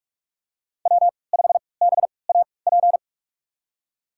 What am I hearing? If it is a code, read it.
Morse code, heard as a keyed tone, about 47 wpm, spaced as for 26 wpm, 700 Hz. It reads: W5BAP
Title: USA Call Signs - 5 Characters - 3x Character Spacing - 45wpm